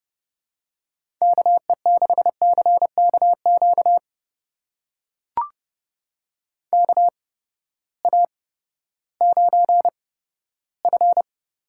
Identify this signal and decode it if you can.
Morse code, heard as a keyed tone, about 30 wpm, spaced as for 15 wpm, 700 Hz. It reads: KE6CKQ KA9F
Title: USA Call Signs - 6 Characters - 8x Character Spacing - 30wpm